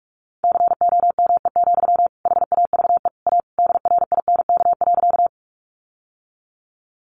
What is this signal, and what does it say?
Morse code, heard as a keyed tone, about 44 wpm, 700 Hz. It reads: COME, HAVE A DRINK.
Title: Sentences from Top 1500 Words 45wpm